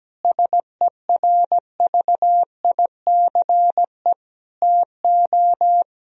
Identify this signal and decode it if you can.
Morse code, heard as a keyed tone, about 17 wpm, 700 Hz. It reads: SERVICE TO
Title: Sentences from Top 1000 Words - Review Part 4 - 17wpm